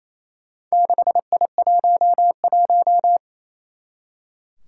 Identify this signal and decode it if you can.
Morse code, heard as a keyed tone, about 27 wpm, 700 Hz. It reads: 6I11
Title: Four Challenging Letter-Number Combinations 28wpm